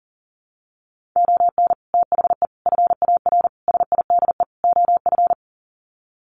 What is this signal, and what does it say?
Morse code, heard as a keyed tone, about 40 wpm, 700 Hz. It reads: ON THE FAR SIDE OF
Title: Sets of 5 Words 40wpm